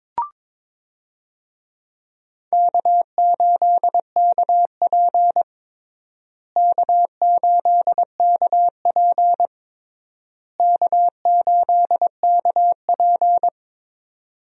Morse code, heard as a keyed tone, about 22 wpm, 700 Hz. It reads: K8KP K8KP K8KP
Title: USA Call Signs - 4 Characters - Repeated 3x - 22wpm